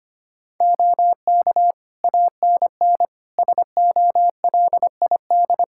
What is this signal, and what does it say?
Morse code, heard as a keyed tone, about 25 wpm, 700 Hz. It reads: OK ANN SOLID
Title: K1USN SST QSO Practice 25wpm